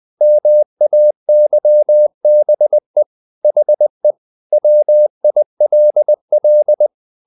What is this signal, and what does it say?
Morse code, heard as a keyed tone, about 20 wpm, 600 Hz. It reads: MAYBE HE WILL
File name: the old man and the sea / the old man and the sea part 1 20wpm